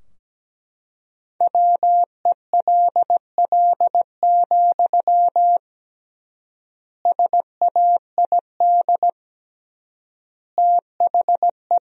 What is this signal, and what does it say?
Morse code, heard as a keyed tone, about 17 wpm, 700 Hz. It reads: WELL, SAID THE
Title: 3x Word-Spacing - Sentences from Top 400 Words - 17wpm